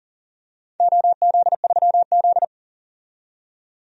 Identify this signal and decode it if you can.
Morse code, heard as a keyed tone, about 38 wpm, 700 Hz. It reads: OZ3Z
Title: DXCC-100 Call Signs - 4 Characters - 40wpm